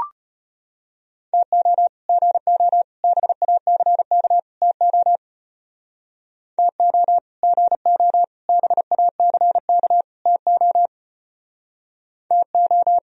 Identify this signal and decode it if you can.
Morse code, heard as a keyed tone, about 32 wpm, 700 Hz. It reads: TO GO BACK TO TO GO BACK TO TO